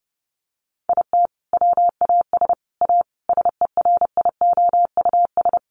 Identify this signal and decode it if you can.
Morse code, heard as a keyed tone, about 30 wpm, 700 Hz. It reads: IT WAS A SERIOUS
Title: Sentences from Top 1100 Words 30wpm